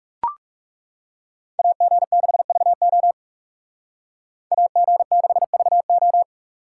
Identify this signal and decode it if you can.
Morse code, heard as a keyed tone, about 38 wpm, 700 Hz. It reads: AG6VO AG6VO